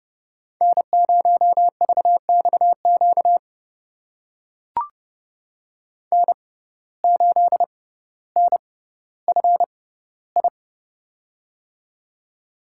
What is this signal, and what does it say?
Morse code, heard as a keyed tone, about 29 wpm, spaced as for 15 wpm, 700 Hz. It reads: N0VXQ N8NFI
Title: USA Call Signs - 5 Characters - 6x Character Spacing - 30wpm